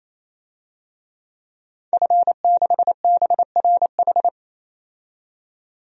Morse code, heard as a keyed tone, about 28 wpm, 700 Hz. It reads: F6BRH
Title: DXCC-100 Call Signs - 5 Characters - 28wpm